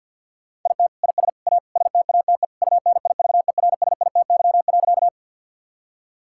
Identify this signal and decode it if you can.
Morse code, heard as a keyed tone, about 50 wpm, 700 Hz. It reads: IT IS A STATE UNIVERSITY.
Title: Sentences from Top 1000 Words - Review Part 5 - 50wpm